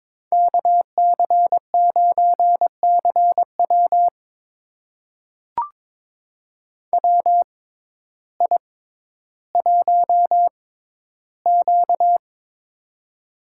Morse code, heard as a keyed tone, about 22 wpm, spaced as for 12 wpm, 700 Hz. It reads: KC9CW WI1Q
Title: USA Call Signs - 6x Character Spacing - 22wpm